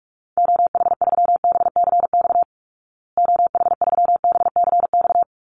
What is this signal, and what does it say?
Morse code, heard as a keyed tone, about 44 wpm, 700 Hz. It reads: OH3BCX OH3BCX
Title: Call Signs Repeated 3x - 45wpm